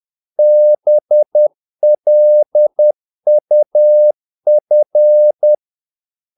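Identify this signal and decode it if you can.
Morse code, heard as a keyed tone, about 10 wpm, 600 Hz. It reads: BLUF
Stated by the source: Morse practice of short Danish words at 10 wpm, from CW med Gnister.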